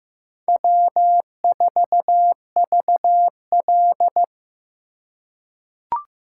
Morse code, heard as a keyed tone, about 15 wpm, 700 Hz. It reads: W4VL
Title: USA Call Signs - 4x Character Spacing - 15wpm